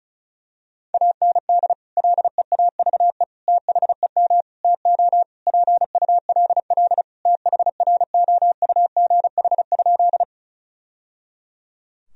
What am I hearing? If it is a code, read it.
Morse code, heard as a keyed tone, about 36 wpm, 700 Hz. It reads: AND LEAVE THEM TO PULL THROUGH?
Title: Sentences from Top 1000 Words - Review Part 2 - 35wpm